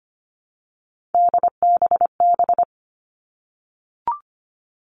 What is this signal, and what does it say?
Morse code, heard as a keyed tone, about 25 wpm, 700 Hz. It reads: DBB
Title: All Cut Numbers - 3-Digit 25wpm